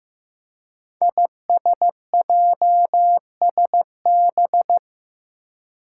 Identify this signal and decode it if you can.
Morse code, heard as a keyed tone, about 15 wpm, 700 Hz. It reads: ISJSB
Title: Five Challenging Letter-Number Combinations 15wpm